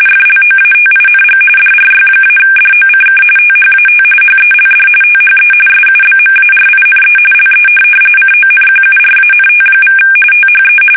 RTTY, about 45.45 baud. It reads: FLOATING DRY-DOCK THAT HELD THE SUBMARINE LE CONQURANT= THE
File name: fsk850 / USS-Massachusetts